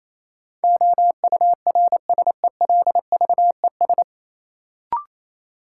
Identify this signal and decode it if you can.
Morse code, heard as a keyed tone, about 28 wpm, 700 Hz. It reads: OURSELVES
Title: Sentences from Top 1600 Words - New Word Drill - 28wpm